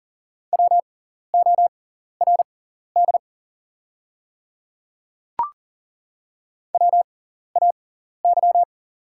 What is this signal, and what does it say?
Morse code, heard as a keyed tone, about 40 wpm, spaced as for 14 wpm, 690 Hz. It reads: WORD WAY
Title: 6x Character Spacing - Top 200 Words - 40wpm